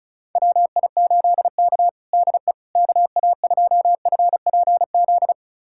Morse code, heard as a keyed tone, about 36 wpm, 700 Hz. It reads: WI8K DE KA2FPZ
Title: USA Callsign DE Callsign 35wpm